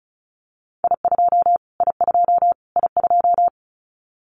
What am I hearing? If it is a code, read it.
Morse code, heard as a keyed tone, about 34 wpm, 700 Hz. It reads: I2 I2 I2
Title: Callsign Prefixes for Top 25 DXCC - Repeated 3x - 35wpm